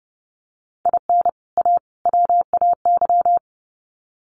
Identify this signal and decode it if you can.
Morse code, heard as a keyed tone, about 30 wpm, 700 Hz. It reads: IN A WAY
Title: Sets of 3 Words 30wpm